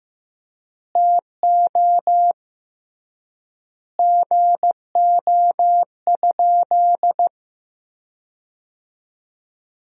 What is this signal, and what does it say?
Morse code, heard as a keyed tone, about 15 wpm, 700 Hz. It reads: TO GO?
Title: 3x Word-Spacing - Sentences from Top 300 Words - 15wpm